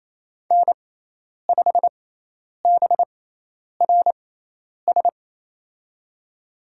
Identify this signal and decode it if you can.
Morse code, heard as a keyed tone, about 28 wpm, spaced as for 10 wpm, 700 Hz. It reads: N5BRS
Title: USA Call Signs - 6x Character Spacing - 28wpm